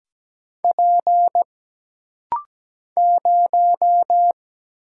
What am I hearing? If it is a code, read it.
Morse code, heard as a keyed tone, about 17 wpm, 700 Hz. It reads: P 0
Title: Single Letter-Number - Rapid-Fire 17wpm